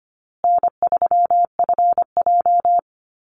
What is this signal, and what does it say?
Morse code, heard as a keyed tone, about 25 wpm, 700 Hz. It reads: N3FJ
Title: USA Call Signs Repeated 3x 25wpm